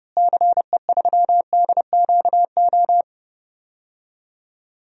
Morse code, heard as a keyed tone, about 32 wpm, 700 Hz. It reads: CE3DQO
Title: DXCC-100 Call Signs - 6 Characters - Repeated 2x - 30wpm